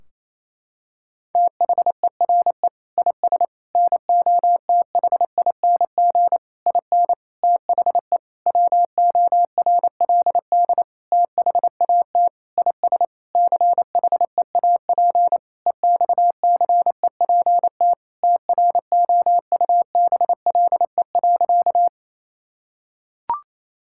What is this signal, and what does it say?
Morse code, heard as a keyed tone, about 28 wpm, 700 Hz. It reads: THERE IS NOTHING IN THE WORLD THAT IS CHEAP EXCEPT TROUBLE.
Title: Sentences from Top 1600 Words 28wpm